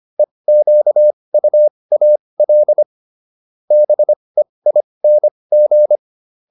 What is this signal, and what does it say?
Morse code, heard as a keyed tone, about 25 wpm, spaced as for 20 wpm, 600 Hz. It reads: EQUAL BEING